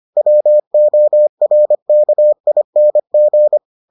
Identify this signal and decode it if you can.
Morse code, heard as a keyed tone, about 25 wpm, 600 Hz. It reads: WORKING